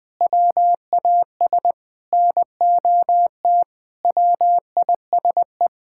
Morse code, heard as a keyed tone, about 21 wpm, 700 Hz. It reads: WAS NOT WISE
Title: Sentences from Top 1600 Words 20wpm